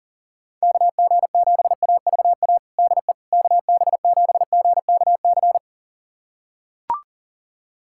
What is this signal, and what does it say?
Morse code, heard as a keyed tone, about 40 wpm, 700 Hz. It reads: KG7AVA DE KB7GKC
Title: USA Callsign DE Callsign - 6 Characters - 40wpm